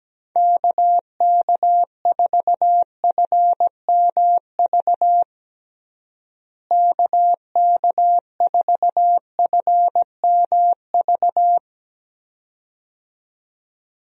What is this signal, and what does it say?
Morse code, heard as a keyed tone, about 17 wpm, 700 Hz. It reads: KK4FMV KK4FMV
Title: USA Call Signs - 6 Characters - Repeated 2x - 17wpm